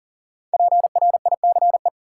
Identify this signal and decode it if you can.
Morse code, heard as a keyed tone, about 40 wpm, 700 Hz. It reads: PRICE
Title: Sentences from Top 600 Words - New Word Drill - 40wpm